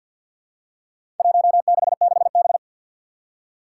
Morse code, heard as a keyed tone, about 51 wpm, 700 Hz. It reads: 166B
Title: Four Challenging Letter-Number Combinations 50wpm